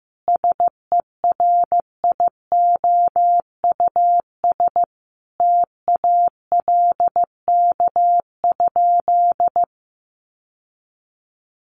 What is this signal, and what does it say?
Morse code, heard as a keyed tone, about 15 wpm, 700 Hz. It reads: SERIOUS TALK?
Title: Sentences from Top 1100 Words 15wpm